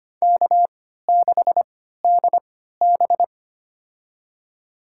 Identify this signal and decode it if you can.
Morse code, heard as a keyed tone, about 25 wpm, spaced as for 14 wpm, 700 Hz. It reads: K6DB
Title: USA Call Signs - 4 Characters - 3x Character Spacing - 25wpm